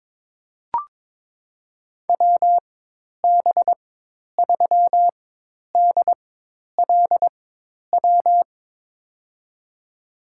Morse code, heard as a keyed tone, about 22 wpm, spaced as for 10 wpm, 700 Hz. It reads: WB3DLW
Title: USA Call Signs - 6 Characters - 4x Character Spacing - 22wpm